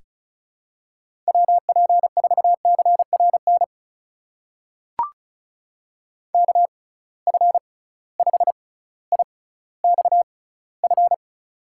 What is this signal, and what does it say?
Morse code, heard as a keyed tone, about 36 wpm, spaced as for 18 wpm, 700 Hz. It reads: WP4CRN KF5IXF
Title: USA Call Signs - 6 Characters - 6x Character Spacing - 35wpm